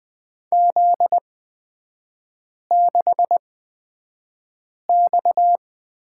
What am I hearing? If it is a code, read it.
Morse code, heard as a keyed tone, about 20 wpm, spaced as for 5 wpm, 700 Hz. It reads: Z6X
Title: 255 Lesson 41e - Period - Three Characters 20wpm